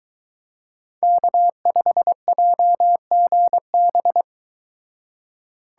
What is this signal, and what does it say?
Morse code, heard as a keyed tone, about 23 wpm, 700 Hz. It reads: K5JGB